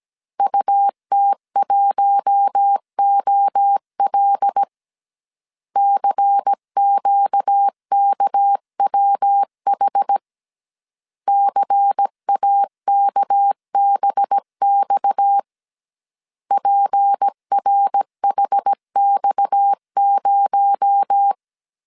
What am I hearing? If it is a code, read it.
Morse code, heard as a keyed tone, about 17 wpm, 800 Hz. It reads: UT1OL CQKWH CAKBX PRHX0
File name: CW / DET / 1 6 1500C S C 16wpm 2733